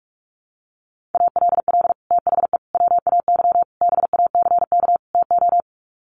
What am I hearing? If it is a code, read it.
Morse code, heard as a keyed tone, about 44 wpm, 700 Hz. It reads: ALL THE WAY BACK TO
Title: Sets of 5 Words Encore 45wpm